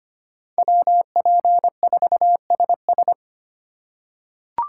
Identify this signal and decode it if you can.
Morse code, heard as a keyed tone, about 25 wpm, 700 Hz. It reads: WP4SS
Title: USA Call Signs - 5 Characters - 4x Character Spacing - 25wpm